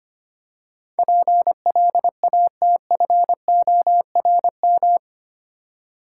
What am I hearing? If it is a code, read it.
Morse code, heard as a keyed tone, about 25 wpm, 700 Hz. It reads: PLATFORM
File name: Eight Letter Words - 25wpm